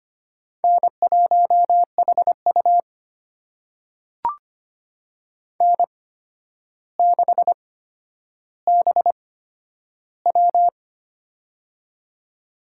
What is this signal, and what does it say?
Morse code, heard as a keyed tone, about 25 wpm, spaced as for 11 wpm, 700 Hz. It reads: N1HU N6BW
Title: USA Call Signs - 4 Characters - 8x Character Spacing - 25wpm